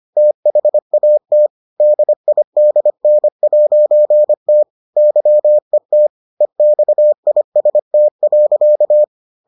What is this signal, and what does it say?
Morse code, heard as a keyed tone, about 25 wpm, 600 Hz. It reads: THAT DIDN'T YET EXIST.